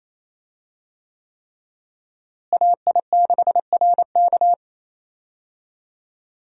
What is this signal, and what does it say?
Morse code, heard as a keyed tone, about 28 wpm, 700 Hz. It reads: AI6RK
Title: USA Call Signs - 5 Characters - Repeated 3x - 28wpm